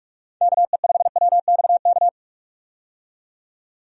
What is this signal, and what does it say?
Morse code, heard as a keyed tone, about 43 wpm, 700 Hz. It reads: KE5WXK
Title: USA Call Signs - 6 Characters - 45wpm